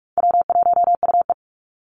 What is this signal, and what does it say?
Morse code, heard as a keyed tone, about 44 wpm, 700 Hz. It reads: W1UE
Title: Call Signs - 45wpm